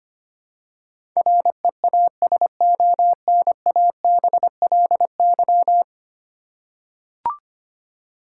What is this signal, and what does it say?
Morse code, heard as a keyed tone, about 25 wpm, 700 Hz. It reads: REASONABLY
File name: Ten Letter Words - 25wpm